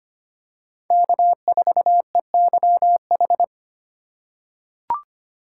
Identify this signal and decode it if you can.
Morse code, heard as a keyed tone, about 25 wpm, 700 Hz. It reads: K4EYH
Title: USA Call Signs - 4x Character Spacing - 25wpm